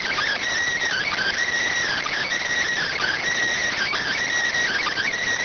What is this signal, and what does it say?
Emergency Alert System (SAME) tones (header bursts), heard as harsh buzzing bursts. It is heard from 0.9 to 5.1 s.